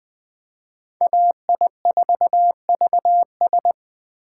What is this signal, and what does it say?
Morse code, heard as a keyed tone, about 21 wpm, 700 Hz. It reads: AI4VS
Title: Speed Racing - USA 2x2 Call Signs 17wpm